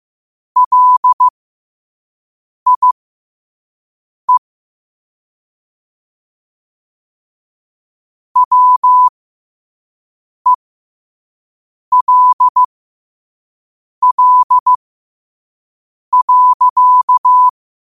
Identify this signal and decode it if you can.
Morse code, heard as a keyed tone, about 15 wpm, spaced as for 5 wpm, 1000 Hz. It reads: LIE WELL.
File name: qotdfiles / 5wpm